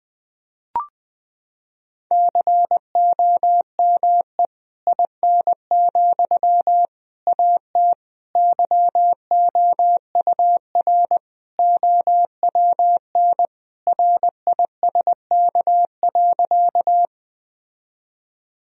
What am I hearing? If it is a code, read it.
Morse code, heard as a keyed tone, about 20 wpm, 700 Hz. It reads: COME IN, AT YOUR OWN RISK.
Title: Sentences from Top 1000 Words - Review Part 4 - 20wpm